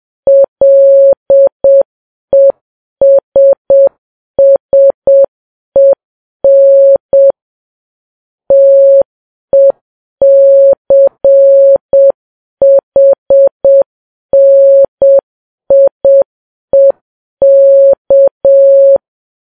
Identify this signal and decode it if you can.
Morse code, heard as a keyed tone, about 7 wpm, 550 Hz. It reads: LESSEN TECHNIEK
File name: Files Web tls / Morse on9fv / morse7kps